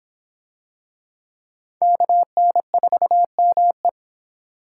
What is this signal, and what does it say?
Morse code, heard as a keyed tone, about 24 wpm, 700 Hz. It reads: KN4ME